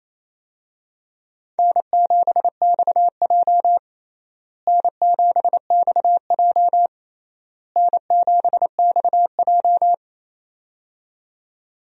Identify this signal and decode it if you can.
Morse code, heard as a keyed tone, about 28 wpm, 700 Hz. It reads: N7XJ N7XJ N7XJ
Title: USA Call Signs - 4 Characters - Repeated 3x - 28wpm